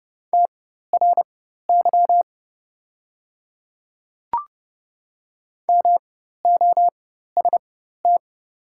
Morse code, heard as a keyed tone, about 30 wpm, spaced as for 14 wpm, 700 Hz. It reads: TRY MOST
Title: 4x Character Spacing - Top 200 Words - 30wpm